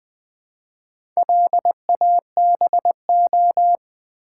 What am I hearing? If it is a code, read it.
Morse code, heard as a keyed tone, about 20 wpm, 700 Hz. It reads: LABO
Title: Build A Word - 2 Letters At A Time - 6-Letter Words - 20wpm